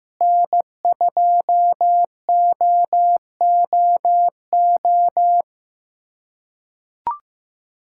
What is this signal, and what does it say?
Morse code, heard as a keyed tone, about 15 wpm, 700 Hz. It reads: N2OOO
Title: USA Callsign DE Callsign - 5 Characters - 15wpm